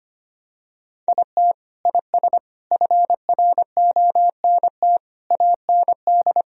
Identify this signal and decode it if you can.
Morse code, heard as a keyed tone, about 25 wpm, 700 Hz. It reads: IT IS FRONT AND
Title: Sentences from Top 1000 Words - Review Part 2 - 25wpm